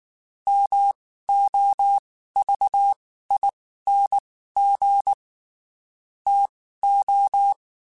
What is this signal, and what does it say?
Morse code, heard as a keyed tone, about 19 wpm, spaced as for 14 wpm, 800 Hz. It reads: MOVING TO
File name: QSOs / QSOs-mp3s / WD0BGP 1@18